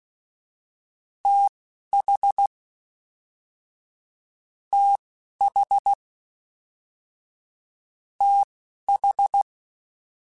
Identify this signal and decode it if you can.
Morse code, heard as a keyed tone, about 16 wpm, spaced as for 11 wpm, 800 Hz. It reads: TH TH TH
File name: TH / TH mp3s / TH Prefixes 3x2 Twice @10